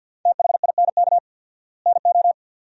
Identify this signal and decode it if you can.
Morse code, heard as a keyed tone, about 52 wpm, 700 Hz. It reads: THINK NO
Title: 4x Word-Spacing - Sentences from Top 200 Words - 50wpm